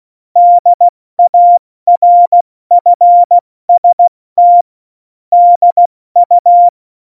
Morse code, heard as a keyed tone, about 16 wpm, 700 Hz. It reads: DARFST DU
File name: Morsekurs / Witze / Witz 22. Dezember 2020, 70 F80